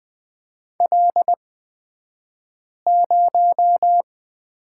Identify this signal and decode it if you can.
Morse code, heard as a keyed tone, about 20 wpm, spaced as for 5 wpm, 700 Hz. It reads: L0
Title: 242 Lesson 39d - Letter z - Two Characters 20wpm